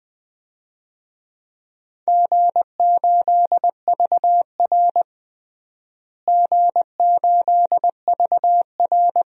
Morse code, heard as a keyed tone, about 20 wpm, 700 Hz. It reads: G8VR G8VR
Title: DXCC-100 Call Signs - 4 Characters - Repeated 2x - 20wpm